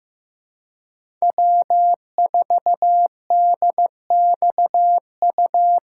Morse code, heard as a keyed tone, about 15 wpm, 700 Hz. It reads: W4DXU
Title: USA Call Signs - 5 Characters - 3x Character Spacing - 15wpm